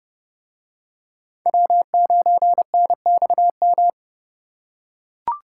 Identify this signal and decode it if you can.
Morse code, heard as a keyed tone, about 29 wpm, 700 Hz. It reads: W9NXM